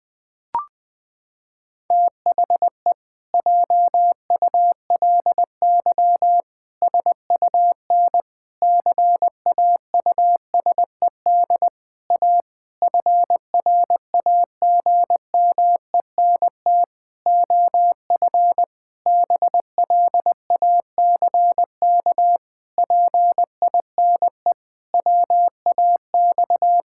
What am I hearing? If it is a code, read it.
Morse code, heard as a keyed tone, about 20 wpm, 700 Hz. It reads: THE JULY SUN CAUSED A FRAGMENT OF BLACK PINE WAX
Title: Pangram Challenge v2 - 20wpm